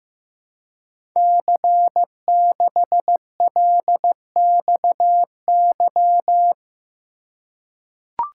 Morse code, heard as a keyed tone, about 15 wpm, 700 Hz. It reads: C6LXY
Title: Code Groups - 5 Letter:Numbers - 15wpm